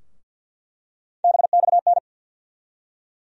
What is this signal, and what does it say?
Morse code, heard as a keyed tone, about 52 wpm, 700 Hz. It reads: BXN
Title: USA Call Sign Postfixes - 3 Letters 50wpm